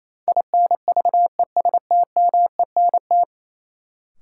Morse code, heard as a keyed tone, about 28 wpm, 700 Hz. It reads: INVESTMENT
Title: Sentences from Top 700 Words - New Word Drill - 28wpm